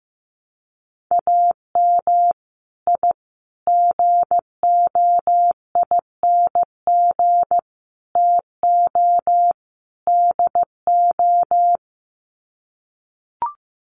Morse code, heard as a keyed tone, about 15 wpm, 700 Hz. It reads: AM I GOING TO DO
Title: Sets of 5 Words Encore 15wpm